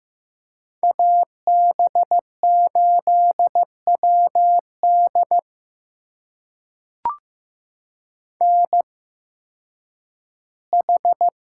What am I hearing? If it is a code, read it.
Morse code, heard as a keyed tone, about 15 wpm, spaced as for 10 wpm, 700 Hz. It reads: AB8WD NH
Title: USA Call Signs - 5 Characters - 8x Character Spacing - 15wpm